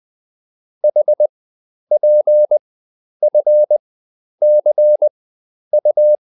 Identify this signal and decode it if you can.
Morse code, heard as a keyed tone, about 20 wpm, spaced as for 10 wpm, 600 Hz. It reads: HPFCU